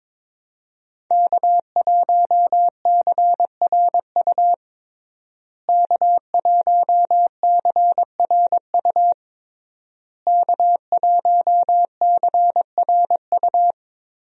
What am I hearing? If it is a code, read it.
Morse code, heard as a keyed tone, about 22 wpm, 700 Hz. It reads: K1CRU K1CRU K1CRU
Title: USA Call Signs - 5 Characters - Repeated 3x - 22wpm